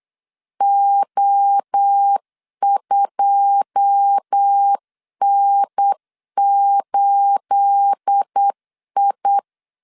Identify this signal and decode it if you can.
Morse code, heard as a keyed tone, about 8 wpm, 800 Hz. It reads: O2N8I
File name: LEZIONI / 9 286 8wpm 3425